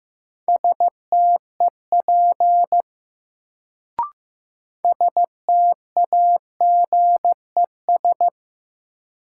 Morse code, heard as a keyed tone, about 16 wpm, 700 Hz. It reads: STEP STAGES
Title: Sets of 5 Words Encore - New Word Drill - 15wpm